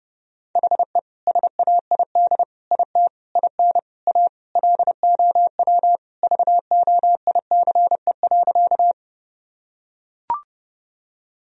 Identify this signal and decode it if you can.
Morse code, heard as a keyed tone, about 30 wpm, 700 Hz. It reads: HE SAID IT IN A LOW VOICE.
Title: Sentences from Top 1000 Words - Review Part 1 - 30wpm